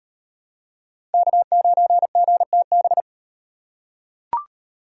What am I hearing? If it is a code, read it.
Morse code, heard as a keyed tone, about 36 wpm, 700 Hz. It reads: K9GTB